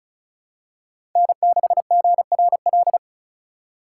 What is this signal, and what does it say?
Morse code, heard as a keyed tone, about 36 wpm, 700 Hz. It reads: N6GRL
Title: USA Call Signs - 5 Characters - 2x Character Spacing - 35wpm